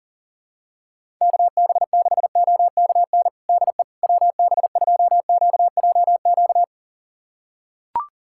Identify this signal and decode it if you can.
Morse code, heard as a keyed tone, about 40 wpm, 700 Hz. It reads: KB6OKN DE WB2QJQ
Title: USA Callsign DE Callsign - 6 Characters - 40wpm